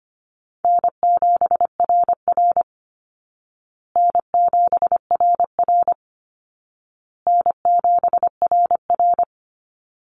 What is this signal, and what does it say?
Morse code, heard as a keyed tone, about 25 wpm, 700 Hz. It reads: N7RR N7RR N7RR
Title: Easy Call Signs Repeated 3x - 25wpm